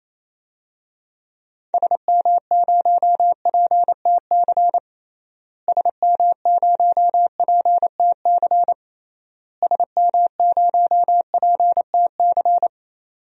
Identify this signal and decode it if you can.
Morse code, heard as a keyed tone, about 28 wpm, 700 Hz. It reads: SM0PTC SM0PTC SM0PTC
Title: DXCC-100 Call Signs - 6 Characters - Repeated 3x - 28wpm